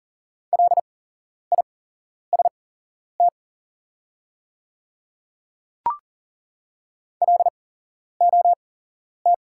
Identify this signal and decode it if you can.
Morse code, heard as a keyed tone, about 40 wpm, spaced as for 11 wpm, 690 Hz. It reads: LIST LOT
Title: 8x Character Spacing - Top 500 Words - 40wpm